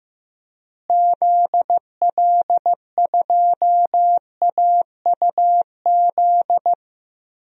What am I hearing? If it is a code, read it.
Morse code, heard as a keyed tone, about 15 wpm, 700 Hz. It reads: ZL2AUZ
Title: DXCC-100 Call Signs - 6 Characters - Repeated 3x - 15wpm